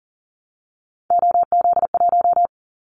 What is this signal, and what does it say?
Morse code, heard as a keyed tone, about 40 wpm, 700 Hz. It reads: OZ1
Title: Callsign Prefixes for Top 100 DXCC - 40wpm